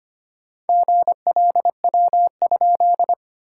Morse code, heard as a keyed tone, about 25 wpm, 690 Hz. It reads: GLW?
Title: SOTA QSO Practice 25wpm